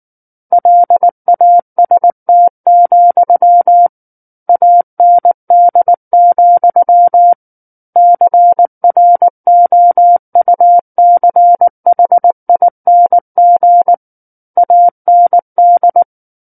Morse code, heard as a keyed tone, about 19 wpm, 700 Hz. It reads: LAST, AND, CROUCHING AND
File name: Book One / 10-Chapter 10 - 19 WPM